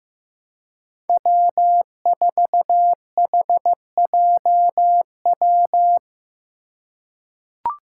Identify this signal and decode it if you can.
Morse code, heard as a keyed tone, about 15 wpm, 700 Hz. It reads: W4HJW
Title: Five Challenging Letter-Number Combinations 15wpm